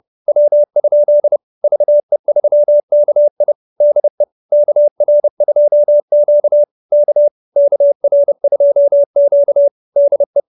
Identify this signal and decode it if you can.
Morse code, heard as a keyed tone, about 30 wpm, 570 Hz. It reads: W? VE3KI DE KR2Q K KR2Q DE